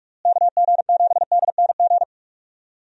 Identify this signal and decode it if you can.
Morse code, heard as a keyed tone, about 46 wpm, 700 Hz. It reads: KG7DNG
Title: USA Call Signs - 6 Characters - Repeated 2x - 45wpm